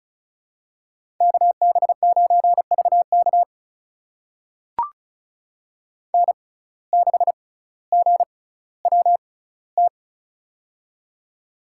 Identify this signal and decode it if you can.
Morse code, heard as a keyed tone, about 36 wpm, spaced as for 18 wpm, 700 Hz. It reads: KB9VK N6GWT
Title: USA Call Signs - 5 Characters - 6x Character Spacing - 35wpm